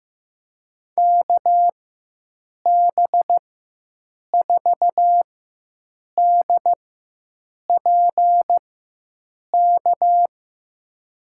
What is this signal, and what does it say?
Morse code, heard as a keyed tone, about 15 wpm, spaced as for 7 wpm, 700 Hz. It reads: KB4DPK
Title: USA Call Signs - 6 Characters - 4x Character Spacing - 15wpm